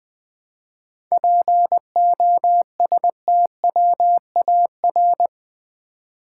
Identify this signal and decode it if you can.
Morse code, heard as a keyed tone, about 20 wpm, 700 Hz. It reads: POSTWAR
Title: 7 Letter Words - First 3 Letters Given - 20wpm